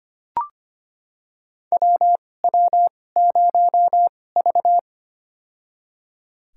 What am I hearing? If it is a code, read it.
Morse code, heard as a keyed tone, about 25 wpm, spaced as for 18 wpm, 700 Hz. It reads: WW0V
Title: USA Call Signs - 4 Characters - 2x Character Spacing - 25wpm